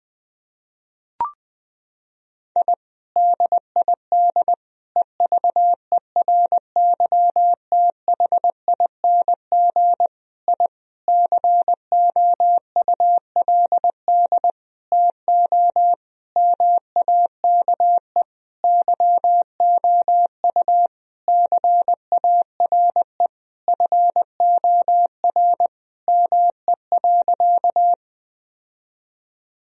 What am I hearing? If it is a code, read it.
Morse code, heard as a keyed tone, about 20 wpm, 700 Hz. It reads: I DID EVERYTHING I COULD TO MAKE YOU CARE FOR ME.
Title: Sentences from Top 1000 Words - Review Part 1 - 20wpm